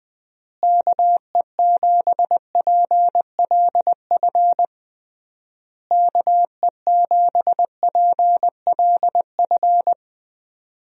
Morse code, heard as a keyed tone, about 20 wpm, 700 Hz. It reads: KE7PLF KE7PLF
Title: USA Call Signs - 6 Characters - Repeated 3x - 20wpm